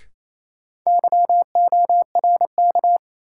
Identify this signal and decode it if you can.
Morse code, heard as a keyed tone, about 28 wpm, 700 Hz. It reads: YORK
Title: Sets of 2 Words - New Word Drill - 28wpm